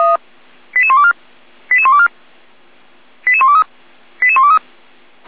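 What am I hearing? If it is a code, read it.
Five-tone selective call (ZVEI), heard as quick runs of five tones, tones of about 70 ms. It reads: 88124 88124 88123 88123
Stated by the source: ZVEI five-tone call 88191 received by SDR.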